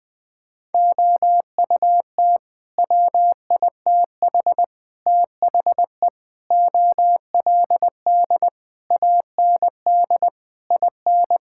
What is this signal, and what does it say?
Morse code, heard as a keyed tone, about 21 wpm, 700 Hz. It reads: OUT WITH THE OLD AND IN
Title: Sets of 9 Words - 20wpm